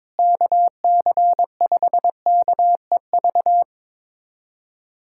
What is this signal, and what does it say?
Morse code, heard as a keyed tone, about 22 wpm, 700 Hz. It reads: KC5KEV
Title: USA Callsign DE Callsign - 6 Characters - 22wpm